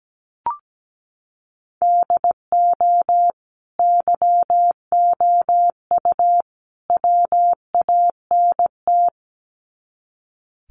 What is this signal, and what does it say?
Morse code, heard as a keyed tone, about 17 wpm, 700 Hz. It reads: DO YOU WANT
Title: Sets of 3 Words 17wpm